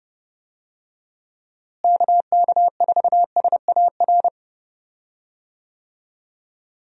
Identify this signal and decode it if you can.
Morse code, heard as a keyed tone, about 30 wpm, 700 Hz. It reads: KK4SAR
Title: USA Call Signs - 6 Characters - Repeated 3x - 30wpm